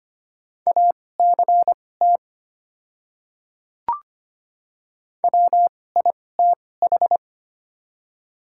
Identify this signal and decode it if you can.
Morse code, heard as a keyed tone, about 25 wpm, spaced as for 18 wpm, 700 Hz. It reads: ACT WITH
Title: 2x Character Spacing - Top 200 Words - 25wpm